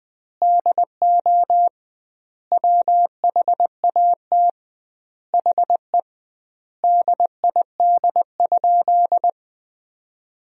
Morse code, heard as a keyed tone, about 21 wpm, 700 Hz. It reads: DO WHAT HE DID?
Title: 2x Word-Spacing - Sentences from Top 100 Words - 20wpm